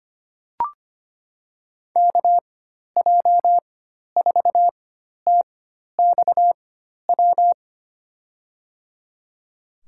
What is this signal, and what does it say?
Morse code, heard as a keyed tone, about 25 wpm, spaced as for 12 wpm, 700 Hz. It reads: KJ4TXW
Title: USA Call Signs - 4x Character Spacing - 25wpm